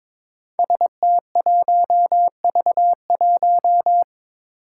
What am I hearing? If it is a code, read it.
Morse code, heard as a keyed tone, about 22 wpm, 700 Hz. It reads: ST1V1
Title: Five Challenging Letter-Number Combinations 22wpm